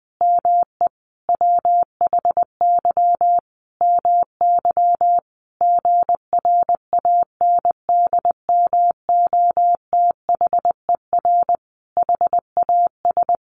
Morse code, heard as a keyed tone, about 20 wpm, 700 Hz. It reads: ME WHY MY GRANDMOTHER HAS
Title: Sentences from Top 1200 Words 20wpm